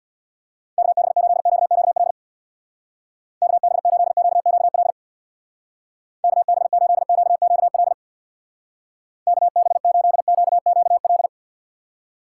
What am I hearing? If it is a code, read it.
Morse code, heard as a keyed tone, about 63 wpm, 700 Hz. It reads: KB8QQL KB8QQL KB8QQL KB8QQL